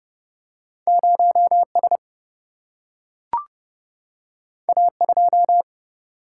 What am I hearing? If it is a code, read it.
Morse code, heard as a keyed tone, about 30 wpm, 700 Hz. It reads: 0 S A 2
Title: Single-Letter Single-Number 30wpm